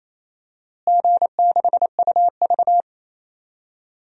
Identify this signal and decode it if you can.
Morse code, heard as a keyed tone, about 28 wpm, 700 Hz. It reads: G6UV
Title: DXCC-100 Call Signs - 4 Characters - Repeated 2x - 28wpm